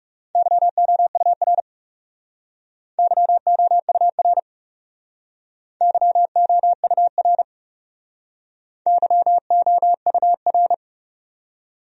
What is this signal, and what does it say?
Morse code, heard as a keyed tone, about 40 wpm, 700 Hz. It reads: YOUR YOUR YOUR YOUR